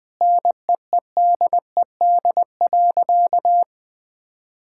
Morse code, heard as a keyed tone, about 20 wpm, 700 Hz. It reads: NEEDED.
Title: Sentences with 6-Word Phrases - 20wpm